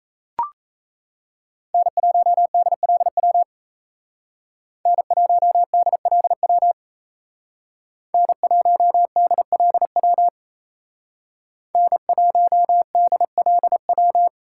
Morse code, heard as a keyed tone, about 36 wpm, 700 Hz. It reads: N1DLW N1DLW N1DLW N1DLW